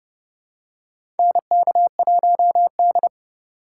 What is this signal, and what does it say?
Morse code, heard as a keyed tone, about 30 wpm, 700 Hz. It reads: NK1D
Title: USA Call Signs - 4 Characters - 2x Character Spacing - 30wpm